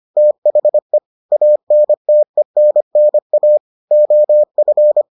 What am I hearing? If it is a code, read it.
Morse code, heard as a keyed tone, about 25 wpm, 600 Hz. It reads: THE ANTENNA OF